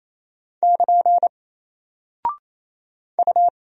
Morse code, heard as a keyed tone, about 28 wpm, 700 Hz. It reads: <KN> U
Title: Single Character (Letter:Number:Punctuation:Prosign) Alt 28wpm